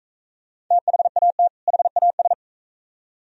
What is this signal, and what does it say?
Morse code, heard as a keyed tone, about 41 wpm, 700 Hz. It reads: THAT HAS